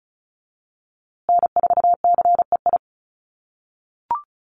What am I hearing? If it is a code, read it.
Morse code, heard as a keyed tone, about 34 wpm, 700 Hz. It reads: N4CEI
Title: USA Call Signs Repeated 2x 35wpm